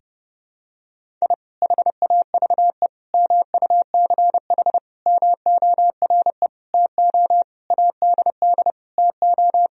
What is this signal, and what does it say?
Morse code, heard as a keyed tone, about 30 wpm, 700 Hz. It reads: I HAVE MUCH MORE TO ADD TO
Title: Sentences from Top 1000 Words - Review Part 1 - 30wpm